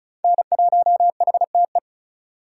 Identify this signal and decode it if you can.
Morse code, heard as a keyed tone, about 45 wpm, 700 Hz. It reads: N1HTE